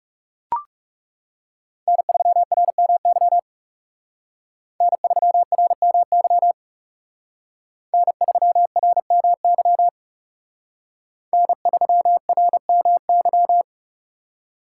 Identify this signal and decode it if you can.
Morse code, heard as a keyed tone, about 36 wpm, 700 Hz. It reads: N3RMY N3RMY N3RMY N3RMY
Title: Speed Racing - USA 1x3 Call Signs 30wpm